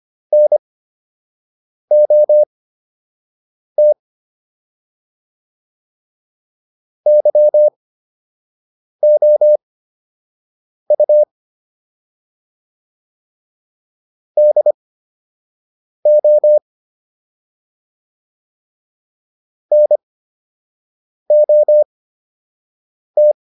Morse code, heard as a keyed tone, about 25 wpm, spaced as for 6 wpm, 600 Hz. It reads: NOT YOU DO NOT